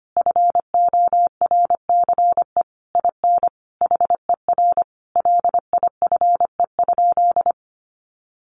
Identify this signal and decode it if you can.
Morse code, heard as a keyed tone, about 25 wpm, 700 Hz. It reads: FORCE IN HER LIFE?
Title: Sentences from Top 1500 Words 25wpm